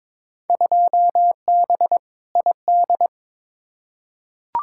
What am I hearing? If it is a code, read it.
Morse code, heard as a keyed tone, about 22 wpm, 700 Hz. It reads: 2B ID
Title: Field Day Exchanges Encore - 22wpm